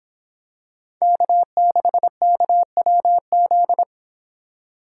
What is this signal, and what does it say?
Morse code, heard as a keyed tone, about 24 wpm, 700 Hz. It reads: K6KWZ